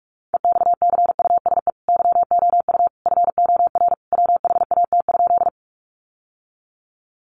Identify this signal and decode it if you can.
Morse code, heard as a keyed tone, about 44 wpm, 700 Hz. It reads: EXCUSE YOU FOR WHAT?
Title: Sentences from Top 700 Words 45wpm